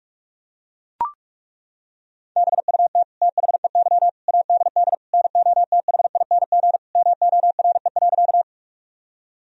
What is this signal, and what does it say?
Morse code, heard as a keyed tone, about 41 wpm, 700 Hz. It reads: BUT THEY ADD NOTHING MORE.
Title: Sentences from Top 1000 Words - Review Part 1 - 45wpm